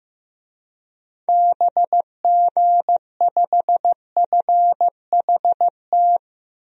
Morse code, heard as a keyed tone, about 15 wpm, 700 Hz. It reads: BG5FHT
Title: DXCC-100 Call Signs - 6 Characters - Repeated 3x - 15wpm